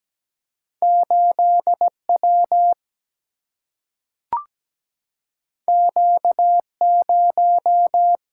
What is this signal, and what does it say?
Morse code, heard as a keyed tone, about 17 wpm, 700 Hz. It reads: 8W Q0
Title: Single-Letter Single-Number 17wpm